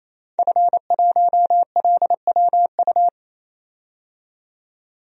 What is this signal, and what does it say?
Morse code, heard as a keyed tone, about 28 wpm, 700 Hz. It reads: F1LWU
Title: DXCC-100 Call Signs - 5 Characters - 28wpm